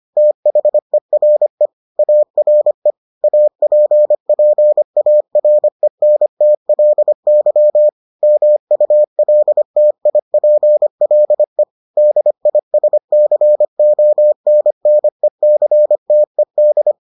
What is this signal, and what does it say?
Morse code, heard as a keyed tone, about 25 wpm, 600 Hz. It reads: THERE ARE APPARENTLY MULTIPLE DISCONNECTED